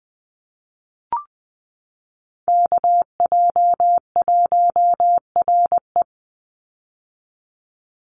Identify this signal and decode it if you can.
Morse code, heard as a keyed tone, about 20 wpm, 700 Hz. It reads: KJ1RE
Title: Call Signs - 20wpm